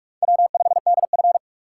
Morse code, heard as a keyed tone, about 46 wpm, 700 Hz. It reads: W5DF
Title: USA Call Signs - 4 Characters - 4x Character Spacing - 45wpm